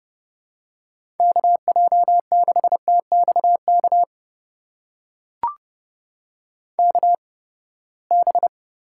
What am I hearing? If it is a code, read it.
Morse code, heard as a keyed tone, about 30 wpm, spaced as for 21 wpm, 700 Hz. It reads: KJ6TXK KB
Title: USA Call Signs - 6 Characters - 8x Character Spacing - 30wpm